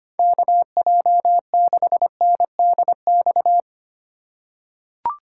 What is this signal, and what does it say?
Morse code, heard as a keyed tone, about 25 wpm, 700 Hz. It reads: KJ6NDX
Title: USA Call Signs - 6 Characters - 2x Character Spacing - 25wpm